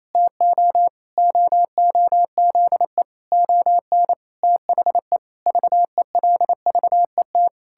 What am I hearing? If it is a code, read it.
Morse code, heard as a keyed tone, about 28 wpm, 700 Hz. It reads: TO OOZE ON THE VELVET
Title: Pangram Challenge v2 - 28wpm